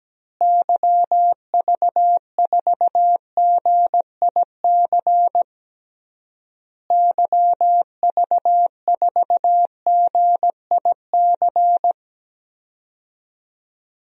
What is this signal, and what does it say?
Morse code, heard as a keyed tone, about 17 wpm, 700 Hz. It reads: YV4GIC YV4GIC
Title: DXCC-100 Call Signs - 6 Characters - Repeated 3x - 17wpm